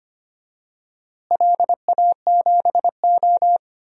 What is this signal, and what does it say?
Morse code, heard as a keyed tone, about 25 wpm, 700 Hz. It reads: LA7O
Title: DXCC-100 Call Signs - 4 Characters - Repeated 3x - 25wpm